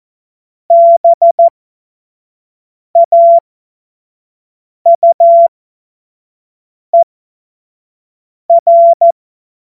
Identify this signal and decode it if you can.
Morse code, heard as a keyed tone, about 14 wpm, spaced as for 5 wpm, 680 Hz. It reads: BAUER
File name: Witz-05-August-2025-25-F70